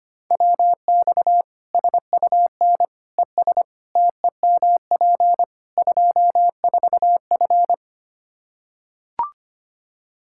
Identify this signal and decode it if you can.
Morse code, heard as a keyed tone, about 25 wpm, 700 Hz. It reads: WX SUN ES TEMP 24F
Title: Practice QSO - Exchange 2 - Info - 25wpm